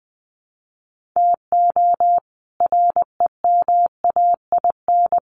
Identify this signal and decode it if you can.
Morse code, heard as a keyed tone, about 20 wpm, 700 Hz. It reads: TO REMAIN
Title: Sentences from Top 1500 Words 20wpm